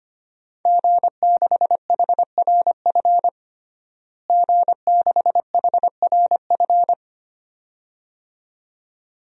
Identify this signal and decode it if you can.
Morse code, heard as a keyed tone, about 25 wpm, 700 Hz. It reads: G6HRF G6HRF
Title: DXCC-100 Call Signs - 5 Characters - Repeated 2x - 25wpm